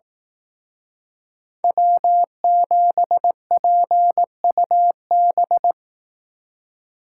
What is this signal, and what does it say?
Morse code, heard as a keyed tone, about 18 wpm, 700 Hz. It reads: W7PUB
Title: Speed Racing - USA 1x3 Call Signs 15wpm